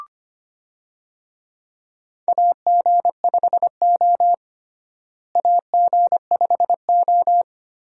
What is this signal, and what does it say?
Morse code, heard as a keyed tone, about 25 wpm, 700 Hz. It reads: AG5O AG5O
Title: USA Call Signs - 4 Characters - Repeated 2x - 25wpm